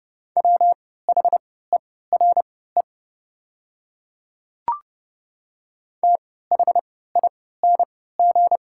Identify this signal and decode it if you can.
Morse code, heard as a keyed tone, about 30 wpm, spaced as for 17 wpm, 700 Hz. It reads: WHERE THING
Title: 3x Character Spacing - Top 200 Words - 30wpm